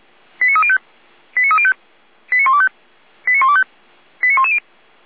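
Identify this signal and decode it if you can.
Five-tone selective call (ZVEI), heard as quick runs of five tones, tones of about 70 ms. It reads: 89305 89305 89125 89125 89110
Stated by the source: ZVEI five-tone selective call received via SDR, from the Feuerwehr Orlamünde audio server.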